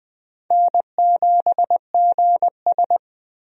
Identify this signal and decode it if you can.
Morse code, heard as a keyed tone, about 20 wpm, 700 Hz. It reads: N7GS
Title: USA Call Signs - 4 Characters - Repeated 2x - 20wpm